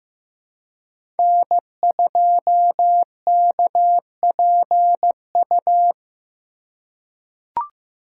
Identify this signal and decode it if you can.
Morse code, heard as a keyed tone, about 15 wpm, 700 Hz. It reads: N2KPU
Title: USA Call Signs - 5 Characters - 2x Character Spacing - 15wpm